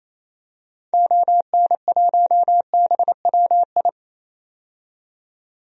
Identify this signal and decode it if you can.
Morse code, heard as a keyed tone, about 28 wpm, 700 Hz. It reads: ON1BWI
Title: DXCC-100 Call Signs - 6 Characters - 28wpm